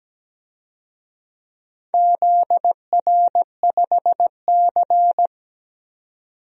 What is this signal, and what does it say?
Morse code, heard as a keyed tone, about 17 wpm, 700 Hz. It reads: ZR5C
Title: DXCC-100 Call Signs - 4 Characters - 17wpm